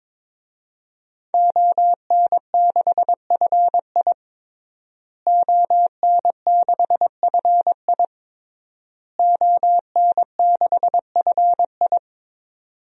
Morse code, heard as a keyed tone, about 21 wpm, 700 Hz. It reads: ON6FI ON6FI ON6FI
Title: DXCC-100 Call Signs - 5 Characters - Repeated 3x - 22wpm